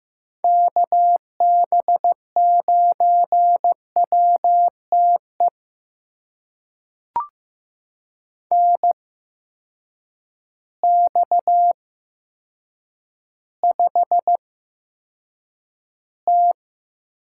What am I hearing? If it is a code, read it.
Morse code, heard as a keyed tone, about 15 wpm, spaced as for 8 wpm, 700 Hz. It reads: KB9WTE NX5T
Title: USA Call Signs - 8x Character Spacing - 15wpm